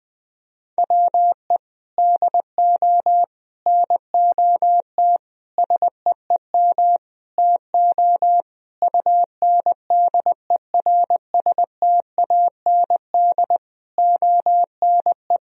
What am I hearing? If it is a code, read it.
Morse code, heard as a keyed tone, about 19 wpm, 700 Hz. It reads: WE DO NOT SEEM TO UNDERSTAND ONE
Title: Sentences from Top 1000 Words - Review Part 1 - 20wpm